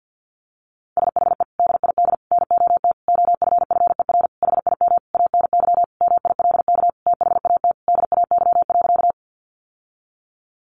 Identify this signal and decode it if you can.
Morse code, heard as a keyed tone, about 50 wpm, 700 Hz. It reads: SHE DID NOT OFFER HIM ANY MILK THAT DAY.
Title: Sentences from Top 1500 Words 50wpm